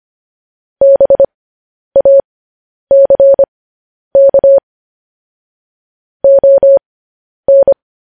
Morse code, heard as a keyed tone, about 25 wpm, spaced as for 10 wpm, 550 Hz. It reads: BACK ON
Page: news headlines in morse code at 10 wpm